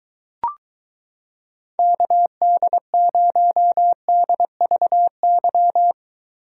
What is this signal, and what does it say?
Morse code, heard as a keyed tone, about 22 wpm, 700 Hz. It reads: KD0DVY